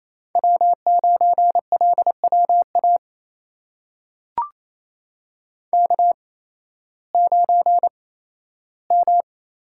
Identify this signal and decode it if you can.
Morse code, heard as a keyed tone, about 28 wpm, spaced as for 15 wpm, 700 Hz. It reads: W9LWA K9M
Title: USA Call Signs - 5 Characters - 8x Character Spacing - 28wpm